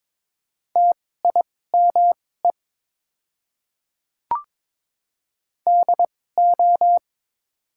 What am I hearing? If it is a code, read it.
Morse code, heard as a keyed tone, about 22 wpm, spaced as for 16 wpm, 700 Hz. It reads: TIME DO
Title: 2x Character Spacing - Top 100 Words - 22wpm